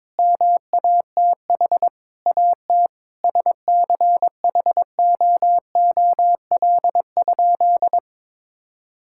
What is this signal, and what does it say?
Morse code, heard as a keyed tone, about 23 wpm, 700 Hz. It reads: MATH AT SCHOOL?
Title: Sentences from Top 1000 Words - Review Part 6 - 22wpm